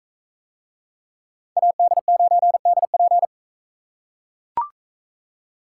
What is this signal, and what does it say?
Morse code, heard as a keyed tone, about 42 wpm, 700 Hz. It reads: AD9DP